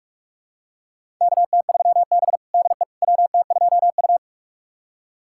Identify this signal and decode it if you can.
Morse code, heard as a keyed tone, about 46 wpm, 700 Hz. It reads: KT3B DE WT2U
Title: USA Callsign DE Callsign - 4 Characters - 45wpm